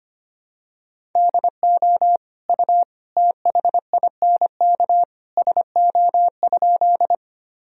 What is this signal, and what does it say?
Morse code, heard as a keyed tone, about 25 wpm, 700 Hz. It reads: DO U THINK SO?
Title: Sentences from Top 200 Words and Abbreviations 25wpm